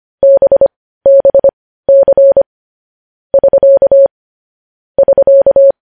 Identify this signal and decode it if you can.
Morse code, heard as a keyed tone, about 25 wpm, spaced as for 15 wpm, 550 Hz. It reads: BBC <SK> <SK>
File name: podcast / News headlines-1x-25x15